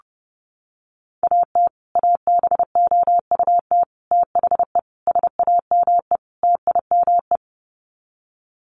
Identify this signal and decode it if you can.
Morse code, heard as a keyed tone, about 28 wpm, 700 Hz. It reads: AT ABOUT THE SAME TIME
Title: Sets of 5 Words Encore 30wpm